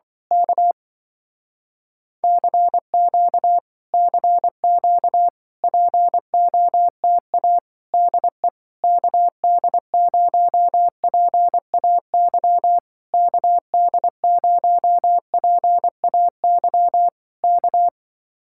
Morse code, heard as a keyed tone, about 24 wpm, 700 Hz. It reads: K CQ CQ POTA DE KD0PAY KD0PAY K
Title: Speed-Racing Common QSO Phrases Encore - 20wpm